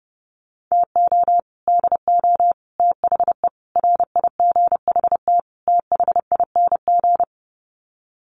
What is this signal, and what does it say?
Morse code, heard as a keyed tone, about 30 wpm, 700 Hz. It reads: TO DO THE RIGHT THING
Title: Sets of 5 Words 30wpm